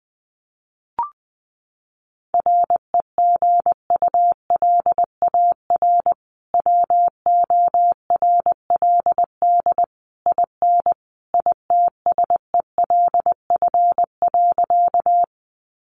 Morse code, heard as a keyed tone, about 20 wpm, 700 Hz. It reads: REGULAR WORLD IN ITSELF.
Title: Sentences from Top 900 Words 20wpm